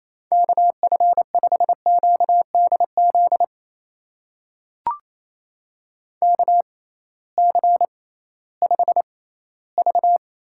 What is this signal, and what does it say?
Morse code, heard as a keyed tone, about 28 wpm, spaced as for 16 wpm, 700 Hz. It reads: KF5QDZ KC5V
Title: USA Call Signs - 6 Characters - 6x Character Spacing - 28wpm